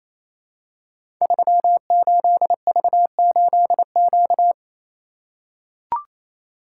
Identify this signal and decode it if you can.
Morse code, heard as a keyed tone, about 28 wpm, 700 Hz. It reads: 38V8Q
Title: Code Groups - 5 Letter:Numbers - 28wpm